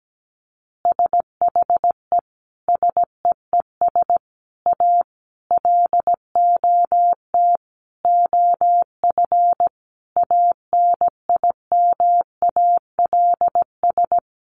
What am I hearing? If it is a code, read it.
Morse code, heard as a keyed tone, about 17 wpm, 700 Hz. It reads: SHE SEES A LOT OF ANIMALS
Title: Easy Phrases 17wpm